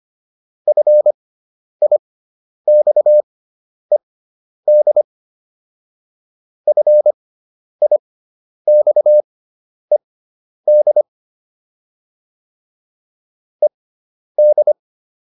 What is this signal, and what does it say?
Morse code, heard as a keyed tone, about 25 wpm, spaced as for 10 wpm, 600 Hz. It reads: FIXED FIXED ED
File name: ED5 10